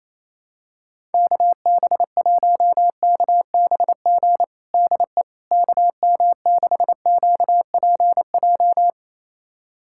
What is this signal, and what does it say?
Morse code, heard as a keyed tone, about 28 wpm, 700 Hz. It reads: KB1KBG DE KM6QPJ
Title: USA Callsign DE Callsign 28wpm